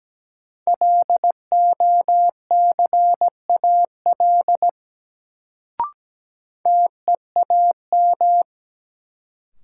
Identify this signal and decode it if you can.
Morse code, heard as a keyed tone, about 17 wpm, 700 Hz. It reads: LOCAL TEAM
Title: Sentences from Top 1500 Words - New Word Drill Alt - 17wpm